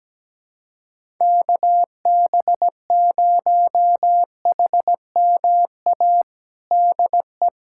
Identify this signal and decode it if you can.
Morse code, heard as a keyed tone, about 17 wpm, 700 Hz. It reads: KB0HMA DE
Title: USA Callsign DE Callsign - 6 Characters - 17wpm